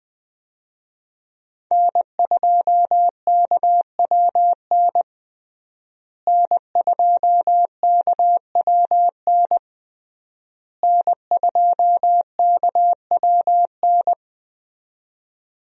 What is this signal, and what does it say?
Morse code, heard as a keyed tone, about 20 wpm, 700 Hz. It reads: N2KWN N2KWN N2KWN
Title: USA Call Signs - 5 Characters - Repeated 3x - 20wpm